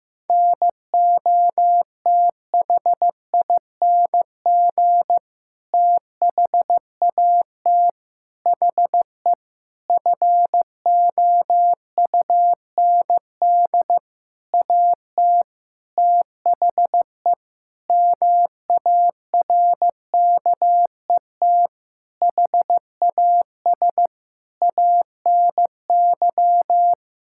Morse code, heard as a keyed tone, about 15 wpm, 700 Hz. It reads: NOTHING THAT HE FOUND AT THE MARKET HAS ANY
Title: Sentences from Top 1000 Words - Review Part 6 - 15wpm